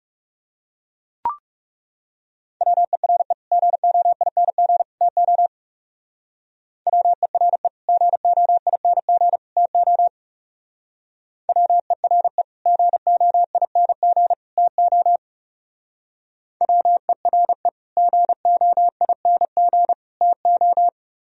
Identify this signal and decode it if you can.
Morse code, heard as a keyed tone, about 35 wpm, 700 Hz. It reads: WERE GOING TO WERE GOING TO WERE GOING TO WERE GOING TO
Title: Speed-Racing - Sets of 3 Words 30wpm